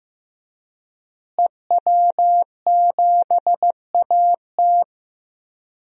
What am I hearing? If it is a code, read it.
Morse code, heard as a keyed tone, about 15 wpm, 700 Hz. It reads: EW7AT
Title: DXCC-100 Call Signs - 5 Characters - Repeated 3x - 15wpm